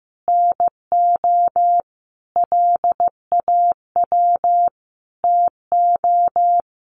Morse code, heard as a keyed tone, about 15 wpm, 700 Hz. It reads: NO LAW TO
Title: Sentences from Top 1100 Words 15wpm